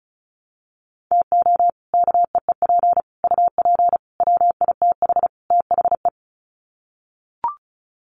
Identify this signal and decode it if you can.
Morse code, heard as a keyed tone, about 34 wpm, 700 Hz. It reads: TO KEEP UP WITH THE
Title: Sets of 5 Words Encore 35wpm